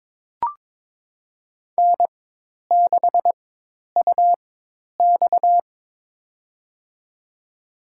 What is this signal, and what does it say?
Morse code, heard as a keyed tone, about 23 wpm, spaced as for 10 wpm, 700 Hz. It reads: N6UX
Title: USA Call Signs - 4 Characters - 4x Character Spacing - 22wpm